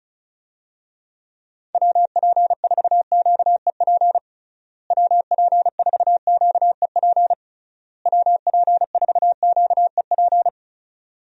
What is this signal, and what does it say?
Morse code, heard as a keyed tone, about 36 wpm, 700 Hz. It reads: WP4QEP WP4QEP WP4QEP
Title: USA Call Signs - 6 Characters - Repeated 3x - 35wpm